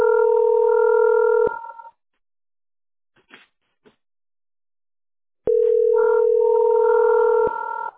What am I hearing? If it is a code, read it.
Ringback tone (North America), 440 + 480 Hz.
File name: e78029de-100e-4318-9a57